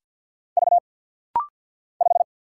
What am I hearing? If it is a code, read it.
Morse code, heard as a keyed tone, about 49 wpm, 700 Hz. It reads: V 5
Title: Single Letter-Number - Rapid-Fire 50wpm